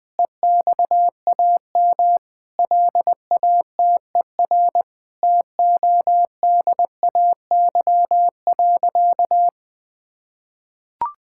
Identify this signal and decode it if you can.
Morse code, heard as a keyed tone, about 20 wpm, 700 Hz. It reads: EXAM LATER TODAY.
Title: Sentences from Top 1000 Words - Review Part 5 - 20wpm